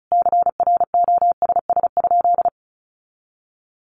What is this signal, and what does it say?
Morse code, heard as a keyed tone, about 34 wpm, 700 Hz. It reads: CROSS?
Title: Sentences from Top 300 Words 35wpm